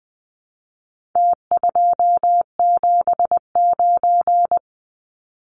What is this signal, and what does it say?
Morse code, heard as a keyed tone, about 19 wpm, 700 Hz. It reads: T279
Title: Numbers - Contest Style - 4-Digit 20wpm